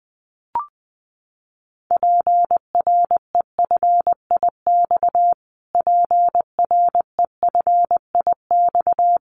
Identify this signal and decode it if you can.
Morse code, heard as a keyed tone, about 19 wpm, 700 Hz. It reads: PREFIX PREFIX
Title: Callsign Prefixes for Top 25 DXCC - Repeated 3x - 20wpm